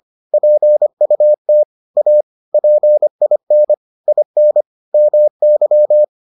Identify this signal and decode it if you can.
Morse code, heard as a keyed tone, about 25 wpm, 600 Hz. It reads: PUT A PIN IN MY